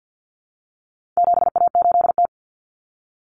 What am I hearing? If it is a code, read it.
Morse code, heard as a keyed tone, about 48 wpm, 700 Hz. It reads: 7A8T
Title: Common Cut Numbers - 4-Digit 50wpm